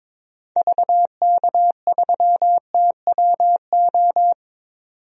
Morse code, heard as a keyed tone, about 22 wpm, 700 Hz. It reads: VK3TWO
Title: DXCC-100 Call Signs - 6 Characters - 22wpm